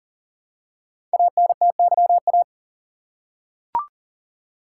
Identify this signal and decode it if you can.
Morse code, heard as a keyed tone, about 39 wpm, 690 Hz. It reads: ANTYA
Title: Code Groups - 5 Letter:Numbers - 40wpm